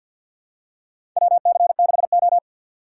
Morse code, heard as a keyed tone, about 52 wpm, 700 Hz. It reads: WC6O
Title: USA Call Signs - 3x Character Spacing - 50wpm